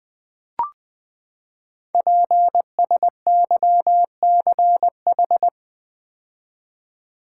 Morse code, heard as a keyed tone, about 20 wpm, 700 Hz. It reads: PSYCH
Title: Build A Word - 5 Letters At A Time - 12-Letter Words - 20wpm